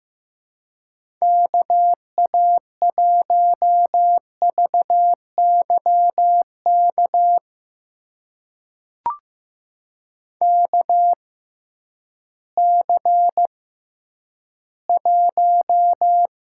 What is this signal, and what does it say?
Morse code, heard as a keyed tone, about 15 wpm, spaced as for 10 wpm, 700 Hz. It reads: KA1VYK KC1
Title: USA Call Signs - 6 Characters - 6x Character Spacing - 15wpm